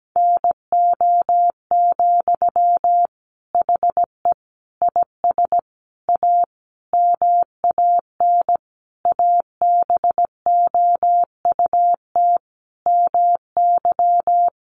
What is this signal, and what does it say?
Morse code, heard as a keyed tone, about 17 wpm, 700 Hz. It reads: NO, HE IS A MAN ABOUT MY
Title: Sentences from Top 400 Words 17wpm